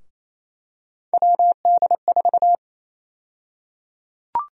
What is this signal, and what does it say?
Morse code, heard as a keyed tone, about 27 wpm, 700 Hz. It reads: WD4
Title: USA Call Sign Prefixes - Repeated 2x - 28wpm